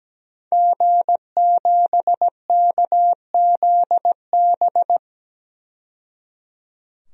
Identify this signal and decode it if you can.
Morse code, heard as a keyed tone, about 17 wpm, 700 Hz. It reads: G7KZB
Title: DXCC-100 Call Signs - 5 Characters - 17wpm